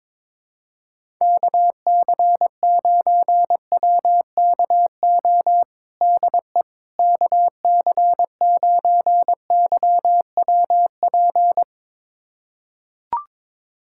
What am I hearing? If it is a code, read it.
Morse code, heard as a keyed tone, about 22 wpm, 700 Hz. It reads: KC9WKO DE KC9YWP
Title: USA Callsign DE Callsign 22wpm